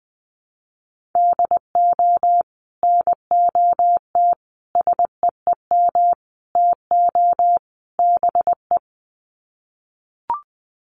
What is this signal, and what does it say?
Morse code, heard as a keyed tone, about 20 wpm, 700 Hz. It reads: DO NOT SEEM TO BE
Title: Sets of 5 Words Encore2 20wpm